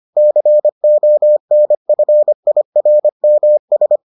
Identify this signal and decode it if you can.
Morse code, heard as a keyed tone, about 25 wpm, 600 Hz. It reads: CONFIRMS